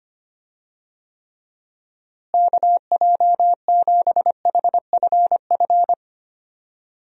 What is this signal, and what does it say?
Morse code, heard as a keyed tone, about 25 wpm, 700 Hz. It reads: KJ7HFF
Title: USA Call Signs - 6 Characters - Repeated 3x - 25wpm